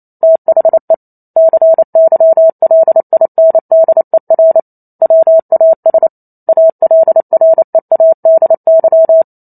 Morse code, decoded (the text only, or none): THE CYLINDER WAS ALREADY